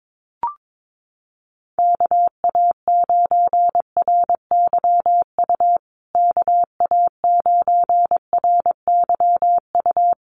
Morse code, heard as a keyed tone, about 22 wpm, 700 Hz. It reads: KA9RYU KA9RYU
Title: USA Call Signs Repeated 2x 22wpm